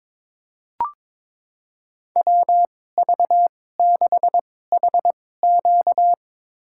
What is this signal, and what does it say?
Morse code, heard as a keyed tone, about 22 wpm, spaced as for 16 wpm, 700 Hz. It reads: WV6HQ
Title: USA Call Signs - 5 Characters - 2x Character Spacing - 22wpm